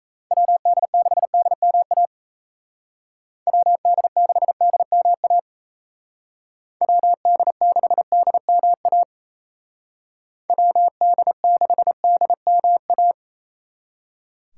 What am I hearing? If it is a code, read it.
Morse code, heard as a keyed tone, about 36 wpm, 700 Hz. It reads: WD6DMA WD6DMA WD6DMA WD6DMA